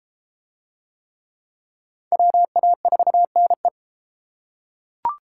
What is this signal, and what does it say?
Morse code, heard as a keyed tone, about 31 wpm, 700 Hz. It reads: WA4NE